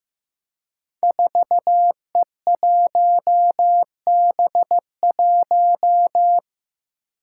Morse code, heard as a keyed tone, about 15 wpm, 700 Hz. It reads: 4E1B1
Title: Five Challenging Letter-Number Combinations 15wpm